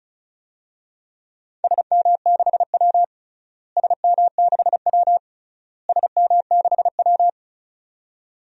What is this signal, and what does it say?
Morse code, heard as a keyed tone, about 34 wpm, 700 Hz. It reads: SM6W SM6W SM6W
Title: DXCC-100 Call Signs - 4 Characters - Repeated 3x - 35wpm